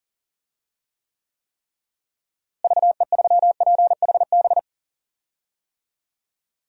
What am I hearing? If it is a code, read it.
Morse code, heard as a keyed tone, about 40 wpm, 700 Hz. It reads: VE3PHB
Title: DXCC-100 Call Signs - 6 Characters - Repeated 3x - 40wpm